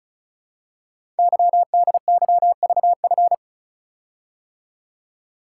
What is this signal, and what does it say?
Morse code, heard as a keyed tone, about 36 wpm, 700 Hz. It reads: YDYVF
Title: Code Groups - 5 Letters - 35wpm